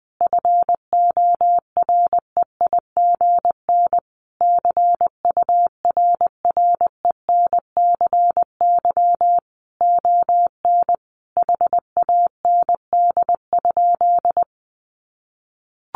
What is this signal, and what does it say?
Morse code, heard as a keyed tone, about 20 wpm, 700 Hz. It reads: FOREIGN CURRENCY ON HAND?
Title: Sentences from Top 1500 Words 20wpm